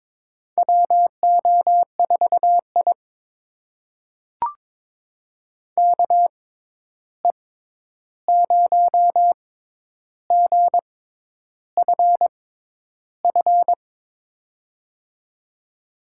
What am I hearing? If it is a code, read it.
Morse code, heard as a keyed tone, about 22 wpm, spaced as for 10 wpm, 700 Hz. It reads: WO4I KE0GFF
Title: USA Call Signs - 6x Character Spacing - 22wpm